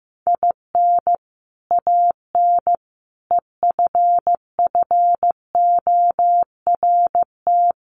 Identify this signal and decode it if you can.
Morse code, heard as a keyed tone, about 15 wpm, 700 Hz. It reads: IN AN EFFORT
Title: Sets of 4 Words 15wpm